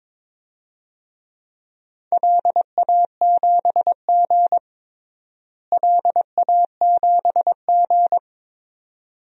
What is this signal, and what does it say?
Morse code, heard as a keyed tone, about 22 wpm, 700 Hz. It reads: LA7G LA7G
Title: DXCC-100 Call Signs - 4 Characters - Repeated 2x - 22wpm